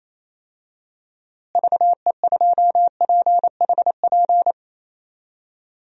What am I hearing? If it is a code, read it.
Morse code, heard as a keyed tone, about 28 wpm, 700 Hz. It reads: VE2PHP
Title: DXCC-100 Call Signs - 6 Characters - Repeated 2x - 28wpm